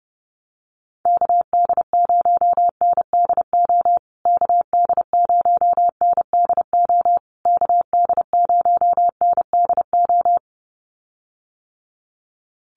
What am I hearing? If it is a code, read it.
Morse code, heard as a keyed tone, about 30 wpm, 700 Hz. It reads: KD0NDO KD0NDO KD0NDO
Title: USA Call Signs Repeated 3x 30wpm